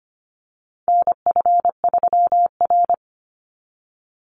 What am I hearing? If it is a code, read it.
Morse code, heard as a keyed tone, about 25 wpm, 700 Hz. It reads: NF3R
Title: Call Signs - 25wpm